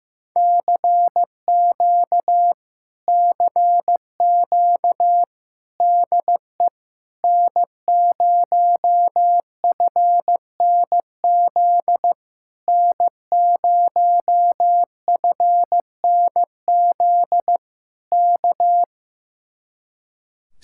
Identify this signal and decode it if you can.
Morse code, heard as a keyed tone, about 15 wpm, 700 Hz. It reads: CQ CQ DE N0FNZ N0FNZ K
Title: Speed-Racing Common QSO Phrases Encore - 15wpm